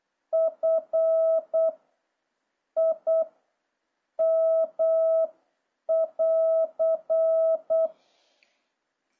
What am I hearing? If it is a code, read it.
Morse code, heard as a keyed tone, about 8 wpm, spaced as for 7 wpm, 650 Hz. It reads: F I M<AR>